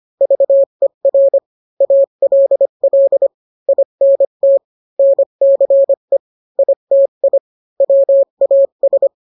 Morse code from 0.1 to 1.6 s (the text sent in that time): VER